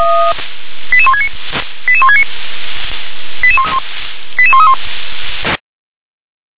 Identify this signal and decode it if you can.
Five-tone selective call (ZVEI), heard as quick runs of five tones, tones of about 80 ms. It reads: 88169 88169 88131
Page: /357/SDR/ZVEI/